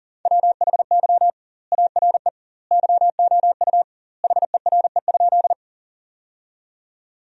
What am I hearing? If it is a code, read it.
Morse code, heard as a keyed tone, about 40 wpm, 700 Hz. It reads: WHY ARE YOU HERE?
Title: 2x Word-Spacing - Sentences from Top 200 Words - 40wpm